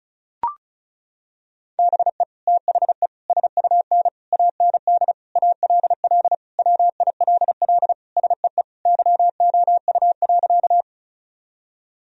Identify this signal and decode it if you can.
Morse code, heard as a keyed tone, about 36 wpm, 700 Hz. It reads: BE THE SUN AND ALL WILL SEE YOU.
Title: Sentences from Top 1000 Words - Review Part 1 - 35wpm